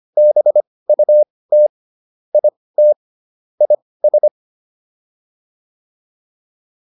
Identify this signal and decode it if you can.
Morse code, heard as a keyed tone, about 25 wpm, spaced as for 18 wpm, 600 Hz. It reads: BUT IT IS